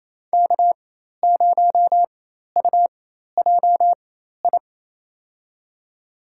Morse code, heard as a keyed tone, about 28 wpm, spaced as for 13 wpm, 700 Hz. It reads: K0UJI
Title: USA Call Signs - 5 Characters - 4x Character Spacing - 28wpm